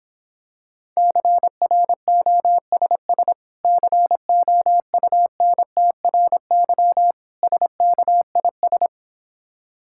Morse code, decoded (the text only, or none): CROSS COUNTRY SKIS